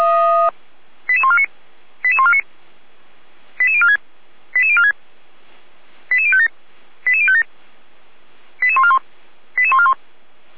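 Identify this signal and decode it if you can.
Five-tone selective call (ZVEI), heard as quick runs of five tones, tones of about 70 ms. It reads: 88149 88149 88046 88046 88057 88057 88141 88141
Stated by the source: ZVEI five-tone selective call 88241 received by SDR.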